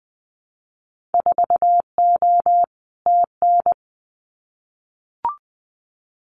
Morse code, heard as a keyed tone, about 20 wpm, 700 Hz. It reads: 4O TN
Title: Winter Field Day Exchanges Encore 20wpm